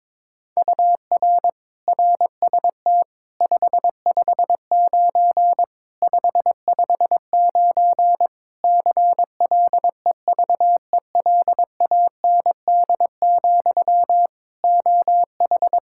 Morse code, heard as a keyed tone, about 22 wpm, 700 Hz. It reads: UR RST 559 559 CLEVELAND, OH
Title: K1USN SST QSO Practice 22wpm